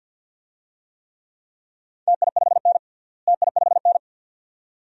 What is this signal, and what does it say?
Morse code, heard as a keyed tone, about 51 wpm, 700 Hz. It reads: TI5N TI5N
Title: DXCC-100 Call Signs - 4 Characters - Repeated 2x - 50wpm